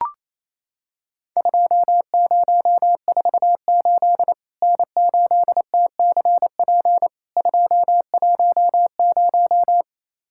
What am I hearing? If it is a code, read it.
Morse code, heard as a keyed tone, about 28 wpm, 700 Hz. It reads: 2048 N8TCP 210
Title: Worked All Europe DX Contest - QTC Exchange - 28wpm